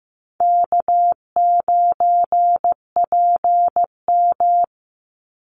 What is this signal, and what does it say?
Morse code, heard as a keyed tone, about 15 wpm, 700 Hz. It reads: K9PM
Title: NA SOTA Activator Callsigns 15wpm